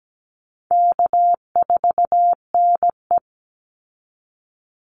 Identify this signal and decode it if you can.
Morse code, heard as a keyed tone, about 17 wpm, 700 Hz. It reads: K4NE
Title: Easy Call Signs Repeated 2x - 17wpm